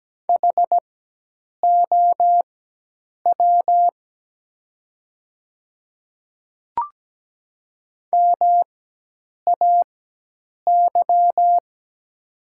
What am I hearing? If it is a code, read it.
Morse code, heard as a keyed tone, about 17 wpm, spaced as for 8 wpm, 700 Hz. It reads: HOW MAY
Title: 4x Character Spacing - Top 100 Words - 17wpm